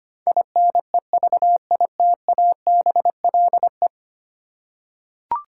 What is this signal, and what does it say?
Morse code, heard as a keyed tone, about 25 wpm, 700 Hz. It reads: INEVITABLE
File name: Ten Letter Words - 25wpm